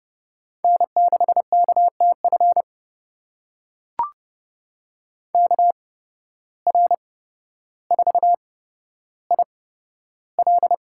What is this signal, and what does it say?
Morse code, heard as a keyed tone, about 30 wpm, spaced as for 13 wpm, 700 Hz. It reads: N6KTF KR4IL
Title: USA Call Signs - 8x Character Spacing - 30wpm